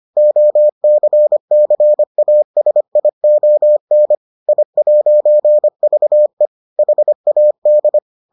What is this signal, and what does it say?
Morse code, heard as a keyed tone, about 25 wpm, 600 Hz. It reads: OCCASION I'VE HAD